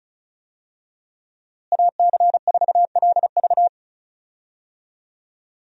Morse code, heard as a keyed tone, about 34 wpm, 700 Hz. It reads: AC4LV
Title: USA Call Signs - 5 Characters - Repeated 2x - 35wpm